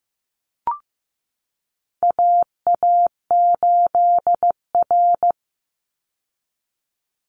Morse code, heard as a keyed tone, about 15 wpm, 700 Hz. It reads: AA8R
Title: Call Signs Repeated 3x - 15wpm